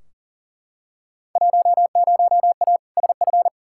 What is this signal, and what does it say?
Morse code, heard as a keyed tone, about 40 wpm, 700 Hz. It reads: 10A SF
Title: Field Day Exchanges Encore - 40wpm